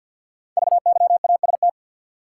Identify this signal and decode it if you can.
Morse code, heard as a keyed tone, about 50 wpm, 700 Hz. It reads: VYAST
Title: Code Groups - 5 Letters - 50wpm